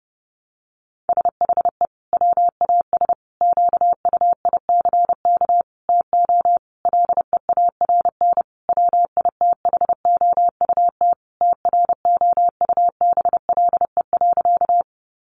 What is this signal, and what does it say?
Morse code, heard as a keyed tone, about 30 wpm, 700 Hz. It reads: SHE WAS QUICK TO LEARN WITHOUT TROUBLE.
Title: Sentences from Top 1200 Words 30wpm